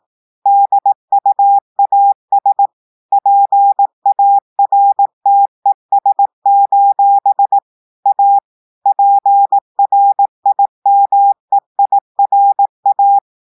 Morse code, heard as a keyed tone, about 18 wpm, 800 Hz. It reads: DUAS PARTES: A PRIMEIRA
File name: audios / portugais / portugais texte en clair 1 vitesse 18 mots minutes